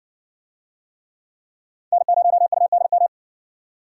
Morse code, heard as a keyed tone, about 54 wpm, 700 Hz. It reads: N0UDM